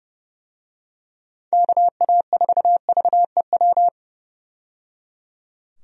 Morse code, heard as a keyed tone, about 30 wpm, 700 Hz. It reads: KA4VEW
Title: USA Call Signs - 6 Characters - 30wpm